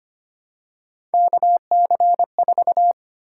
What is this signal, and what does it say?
Morse code, heard as a keyed tone, about 25 wpm, 700 Hz. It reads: KC4
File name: Three Letter-Number Combinations - 25wpm